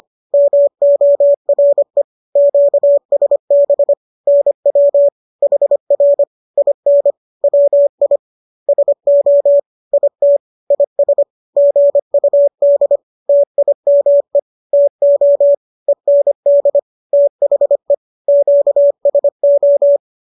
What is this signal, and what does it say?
Morse code, heard as a keyed tone, about 25 wpm, 570 Hz. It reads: MORE QSB NW HR IN WI SO IT IS GUD TIME TO END THE QSO